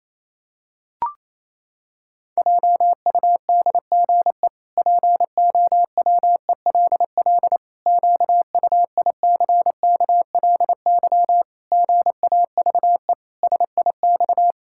Morse code, heard as a keyed tone, about 28 wpm, 700 Hz. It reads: JUDGE POWELL QUICKLY GAVE SIX
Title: Pangram Challenge v2 - 28wpm